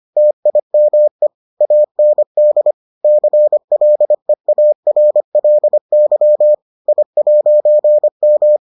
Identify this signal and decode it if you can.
Morse code, heard as a keyed tone, about 25 wpm, 600 Hz. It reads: TIME AND CLEARLY I'M